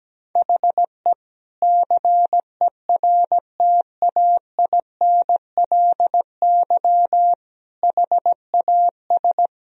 Morse code, heard as a keyed tone, about 17 wpm, 700 Hz. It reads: HE CERTAINLY HAS
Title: Sentences from Top 1000 Words - Review Part 3 - 17wpm